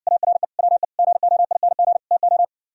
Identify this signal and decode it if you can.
Morse code, heard as a keyed tone, about 60 wpm, 700 Hz. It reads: ARE WE GOING TO